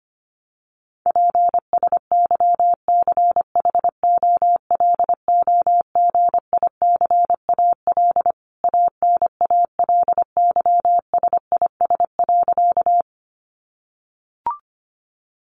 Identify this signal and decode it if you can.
Morse code, heard as a keyed tone, about 25 wpm, 700 Hz. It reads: PSYCHOLOGICAL ANALYSIS.
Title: Sentences from Top 1400 Words 25wpm